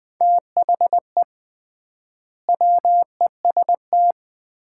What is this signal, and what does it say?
Morse code, heard as a keyed tone, about 20 wpm, 700 Hz. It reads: THE WEST
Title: 3x Word-Spacing - Sets of 5 Words - 20wpm